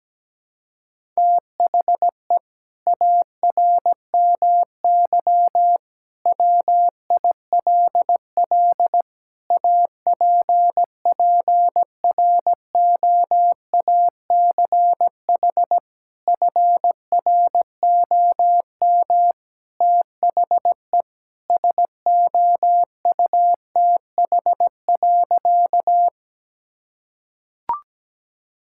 Morse code, heard as a keyed tone, about 17 wpm, 700 Hz. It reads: THE ARMY WILL APPROACH FROM THE SOUTH.
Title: Sentences from Top 1000 Words - Review Part 6 - 17wpm